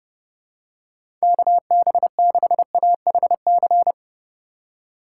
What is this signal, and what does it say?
Morse code, heard as a keyed tone, about 30 wpm, 700 Hz. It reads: KB6AHC
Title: USA Call Signs - 6 Characters - 3x Character Spacing - 30wpm